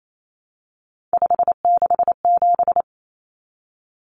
Morse code, heard as a keyed tone, about 28 wpm, 700 Hz. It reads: 567
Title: Numbers - 3-Digit 28wpm